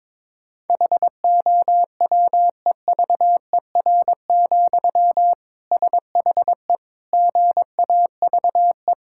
Morse code, heard as a keyed tone, about 22 wpm, 700 Hz. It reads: HOWEVER, SHE GAVE
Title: Sentences from Top 1000 Words - Review Part 6 - 22wpm